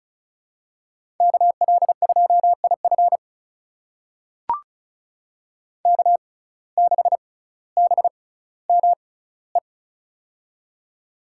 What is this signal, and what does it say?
Morse code, heard as a keyed tone, about 36 wpm, spaced as for 18 wpm, 700 Hz. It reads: KL2IF K6BME
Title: USA Call Signs - 5 Characters - 6x Character Spacing - 35wpm